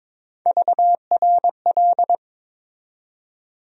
Morse code, heard as a keyed tone, about 21 wpm, 700 Hz. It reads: VRL
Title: USA Call Sign Postfixes - 3 Letters 22wpm